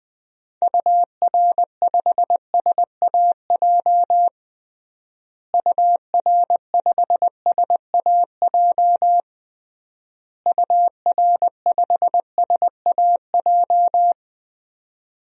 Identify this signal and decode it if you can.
Morse code, heard as a keyed tone, about 21 wpm, 700 Hz. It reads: UR5SAJ UR5SAJ UR5SAJ
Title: DXCC-100 Call Signs - 6 Characters - Repeated 3x - 20wpm